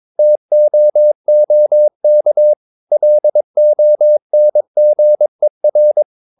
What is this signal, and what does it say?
Morse code, heard as a keyed tone, about 22 wpm, 600 Hz. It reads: TOOK LONGER
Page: Romanen Never af Ken Follet på engelsk.